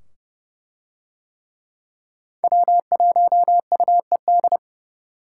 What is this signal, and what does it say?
Morse code, heard as a keyed tone, about 29 wpm, 700 Hz. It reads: W1UED